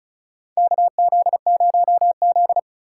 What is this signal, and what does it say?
Morse code, heard as a keyed tone, about 34 wpm, 700 Hz. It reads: KZ0Z
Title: USA Call Signs - 4 Characters - 2x Character Spacing - 35wpm